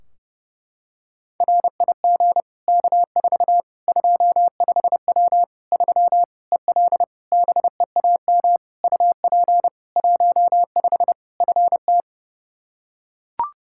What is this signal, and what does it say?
Morse code, heard as a keyed tone, about 30 wpm, 700 Hz. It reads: RIG K4 25W 3 EL BEAM UP 15 FT
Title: Practice QSO - Exchange 2 - Info - 30wpm